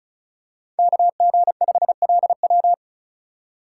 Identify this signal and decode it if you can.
Morse code, heard as a keyed tone, about 34 wpm, 700 Hz. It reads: KG5LW
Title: USA Call Signs - 5 Characters - 4x Character Spacing - 35wpm